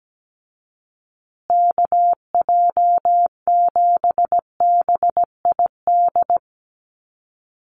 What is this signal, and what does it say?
Morse code, heard as a keyed tone, about 17 wpm, 700 Hz. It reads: KJ7BID
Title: USA Call Signs Repeated 3x 17wpm